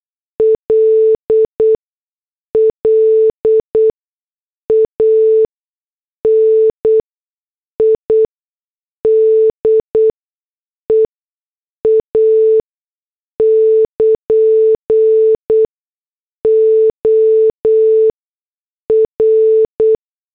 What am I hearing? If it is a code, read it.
Morse code, heard as a keyed tone, about 8 wpm, 440 Hz. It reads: L L A N I D E A <KN> O R
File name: morse / headlines